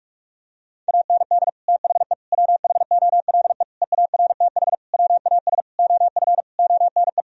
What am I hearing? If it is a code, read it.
Morse code, heard as a keyed tone, about 41 wpm, 700 Hz. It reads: AND THE WHOLE EARTH WAS OF ONE
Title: Sentences from Top 1000 Words - Review Part 3 - 45wpm